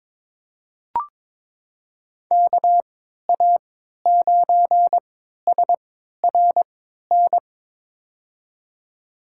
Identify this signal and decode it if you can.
Morse code, heard as a keyed tone, about 22 wpm, spaced as for 12 wpm, 700 Hz. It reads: KA9SRN
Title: USA Call Signs - 6 Characters - 3x Character Spacing - 22wpm